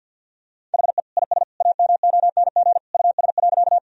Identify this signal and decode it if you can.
Morse code, heard as a keyed tone, about 50 wpm, 700 Hz. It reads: HE IS AMONG US.
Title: Sentences from Top 1000 Words - Review Part 3 - 50wpm